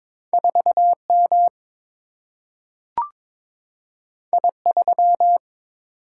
Morse code, heard as a keyed tone, about 22 wpm, 700 Hz. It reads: 4M I3
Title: Single-Letter Single-Number 22wpm